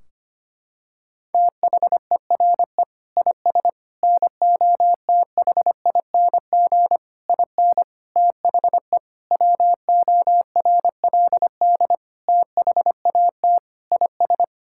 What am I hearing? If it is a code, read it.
Morse code, heard as a keyed tone, about 25 wpm, 700 Hz. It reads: THERE IS NOTHING IN THE WORLD THAT IS
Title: Sentences from Top 1600 Words 25wpm